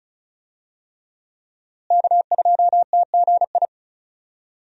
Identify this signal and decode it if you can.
Morse code, heard as a keyed tone, about 36 wpm, 700 Hz. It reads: K2TGI
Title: USA Call Signs - 5 Characters - Repeated 3x - 35wpm